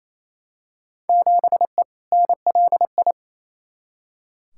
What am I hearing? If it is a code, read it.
Morse code, heard as a keyed tone, about 28 wpm, 700 Hz. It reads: 7E NLI
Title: Field Day Exchanges Encore - 28wpm